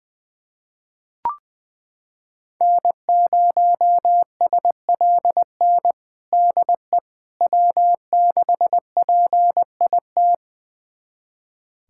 Morse code, heard as a keyed tone, about 20 wpm, 700 Hz. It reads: N0SLN DE W6PIT
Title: USA Callsign DE Callsign - 5 Characters - 20wpm